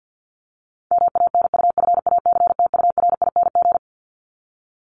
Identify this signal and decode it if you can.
Morse code, heard as a keyed tone, about 50 wpm, 700 Hz. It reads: MANUFACTURING
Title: ING Suffix - Top 100 Words with Base Words 50wpm